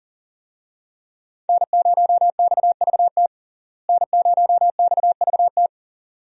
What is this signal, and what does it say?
Morse code, heard as a keyed tone, about 40 wpm, 700 Hz. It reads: N0XVT N0XVT
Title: USA Call Signs - 5 Characters - Repeated 2x - 40wpm